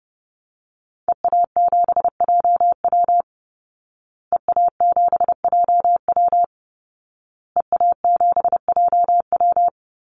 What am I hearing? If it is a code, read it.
Morse code, heard as a keyed tone, about 30 wpm, 700 Hz. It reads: EA7JW EA7JW EA7JW
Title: Call Signs Repeated 2x - 30wpm